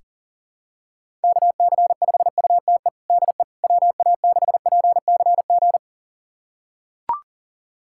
Morse code, heard as a keyed tone, about 40 wpm, 700 Hz. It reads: KC5UTE DE WA6PCG
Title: USA Callsign DE Callsign 40wpm